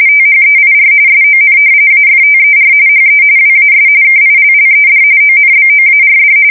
RTTY, about 45.45 baud. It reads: THEIR SON AFTER HE WAS KILLED ONBOARD
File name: fsk170 / battleshipnj